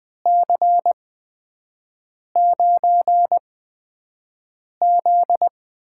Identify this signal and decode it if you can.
Morse code, heard as a keyed tone, about 20 wpm, spaced as for 5 wpm, 700 Hz. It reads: C9Z
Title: USA Call Signs - 6 Characters - 8x Character Spacing - 20wpm